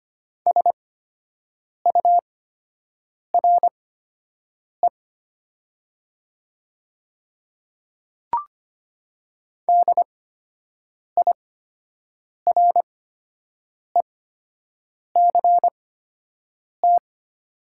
Morse code, heard as a keyed tone, about 25 wpm, spaced as for 7 wpm, 690 Hz. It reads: SURE DIRECT
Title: 8x Character Spacing - Top 500 Words - 25wpm